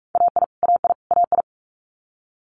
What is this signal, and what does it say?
Morse code, heard as a keyed tone, about 46 wpm, 700 Hz. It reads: AI AI AI
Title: Multi-letter Phonemes Repeated 3x 45wpm